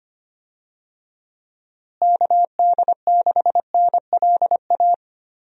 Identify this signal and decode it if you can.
Morse code, heard as a keyed tone, about 25 wpm, 700 Hz. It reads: KD6NLA
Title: USA Call Signs - 6 Characters - Repeated 2x - 25wpm